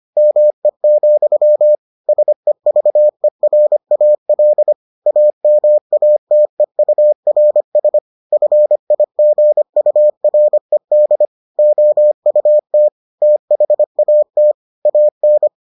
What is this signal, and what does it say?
Morse code, heard as a keyed tone, about 25 wpm, 600 Hz. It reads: ME, SEVERAL AMATEURS FIGURED OUT THAT AN